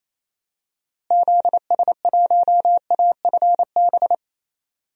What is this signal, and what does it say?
Morse code, heard as a keyed tone, about 28 wpm, 700 Hz. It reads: ZS1AFB
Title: DXCC-100 Call Signs - 6 Characters - Repeated 3x - 28wpm